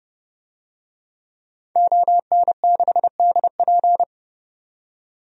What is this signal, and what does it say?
Morse code, heard as a keyed tone, about 30 wpm, 700 Hz. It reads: ON6DP
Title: DXCC-100 Call Signs - 5 Characters - 30wpm